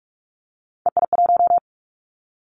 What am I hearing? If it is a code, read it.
Morse code, heard as a keyed tone, about 46 wpm, 700 Hz. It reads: EI1
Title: Callsign Prefixes for Top 50 DXCC - 45wpm